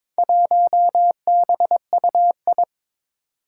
Morse code, heard as a keyed tone, about 22 wpm, 700 Hz. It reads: 1BUI
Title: Four Challenging Letter-Number Combinations 22wpm